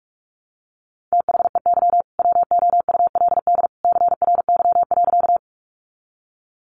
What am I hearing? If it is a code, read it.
Morse code, heard as a keyed tone, about 44 wpm, 700 Hz. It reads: THEY WOULD CRY.
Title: Sentences from Top 500 Words 45wpm